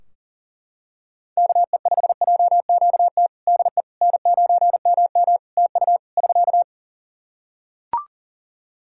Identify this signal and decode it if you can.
Morse code, heard as a keyed tone, about 40 wpm, 700 Hz. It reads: KE5JQT DE N9MM TU <SK>
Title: K1USN SST QSO Practice 40wpm